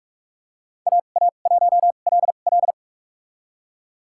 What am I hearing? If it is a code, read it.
Morse code, heard as a keyed tone, about 44 wpm, spaced as for 32 wpm, 700 Hz. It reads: AA1LL
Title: USA Call Signs - 5 Characters - 2x Character Spacing - 45wpm